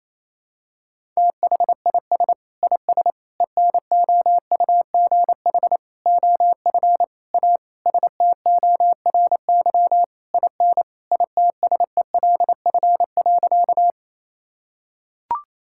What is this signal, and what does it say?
Morse code, heard as a keyed tone, about 28 wpm, 700 Hz. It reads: THIS IS ENOUGH OF A STORY IN ITSELF.
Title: Sentences from Top 1000 Words - Review Part 5 - 28wpm